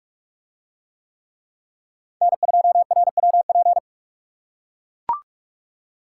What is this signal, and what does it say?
Morse code, heard as a keyed tone, about 44 wpm, 700 Hz. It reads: N2RWP